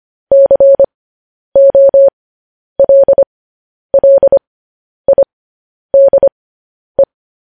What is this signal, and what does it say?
Morse code, heard as a keyed tone, about 25 wpm, spaced as for 10 wpm, 550 Hz. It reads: COLLIDE
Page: news headlines in morse code at 10 wpm